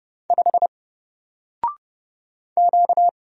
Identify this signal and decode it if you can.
Morse code, heard as a keyed tone, about 30 wpm, 700 Hz. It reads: 5 Q
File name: Single-Letter-Number-30wpm